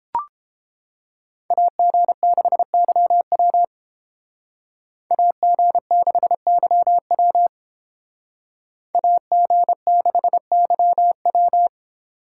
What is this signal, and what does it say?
Morse code, heard as a keyed tone, about 32 wpm, 700 Hz. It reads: AG6YW AG6YW AG6YW